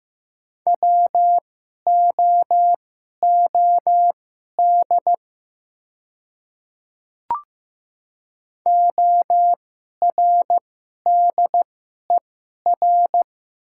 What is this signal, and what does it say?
Morse code, heard as a keyed tone, about 15 wpm, spaced as for 11 wpm, 700 Hz. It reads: WOOD ORDER
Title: 2x Character Spacing - Top 500 Words - 15wpm